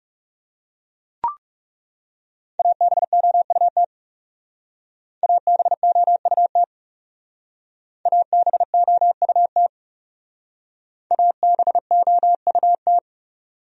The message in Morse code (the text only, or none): ABOUT ABOUT ABOUT ABOUT